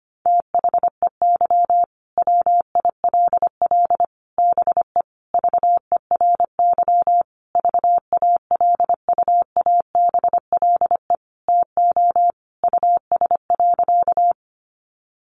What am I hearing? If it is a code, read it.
Morse code, heard as a keyed tone, about 25 wpm, 700 Hz. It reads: THEY WILL BE VERY VALUABLE TO US.
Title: Sentences from Top 1100 Words 25wpm